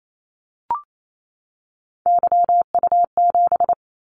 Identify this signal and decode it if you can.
Morse code, heard as a keyed tone, about 28 wpm, 700 Hz. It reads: YU7
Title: Callsign Prefixes for Top 25 DXCC - 28wpm